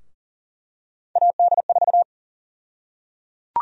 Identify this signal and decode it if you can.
Morse code, heard as a keyed tone, about 40 wpm, 700 Hz. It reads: AD4
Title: USA Call Sign Prefixes - 40wpm